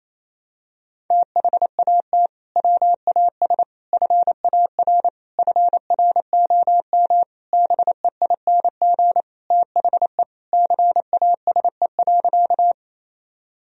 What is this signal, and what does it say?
Morse code, heard as a keyed tone, about 28 wpm, 700 Hz. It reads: THAT WAS FAR FROM BEING THE CASE.
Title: Sentences from Top 1000 Words - Review Part 3 - 28wpm